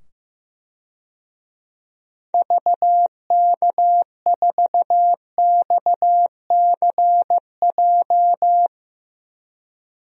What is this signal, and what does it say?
Morse code, heard as a keyed tone, about 15 wpm, 700 Hz. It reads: VK4XCJ
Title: DXCC-100 Call Signs - 6 Characters - 15wpm